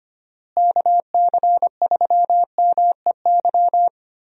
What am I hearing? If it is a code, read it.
Morse code, heard as a keyed tone, about 25 wpm, 700 Hz. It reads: KC3MEY
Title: USA Call Signs - 6 Characters - Repeated 3x - 25wpm